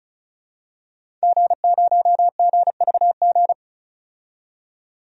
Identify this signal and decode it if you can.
Morse code, heard as a keyed tone, about 36 wpm, 700 Hz. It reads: G0GVG
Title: DXCC-100 Call Signs - 5 Characters - 35wpm